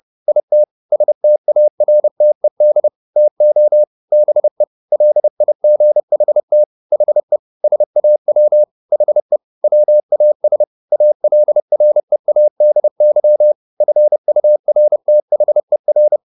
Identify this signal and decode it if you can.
Morse code, heard as a keyed tone, about 30 wpm, 600 Hz. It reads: IT STARTED TO BE LIGHT HE SAW HE WAS ALREADY FURTHER
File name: the old man and the sea / the old man and the sea 1 30wpm